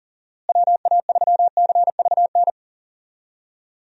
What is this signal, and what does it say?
Morse code, heard as a keyed tone, about 40 wpm, 700 Hz. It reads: WA3CVN
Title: USA Call Signs - 6 Characters - Repeated 3x - 40wpm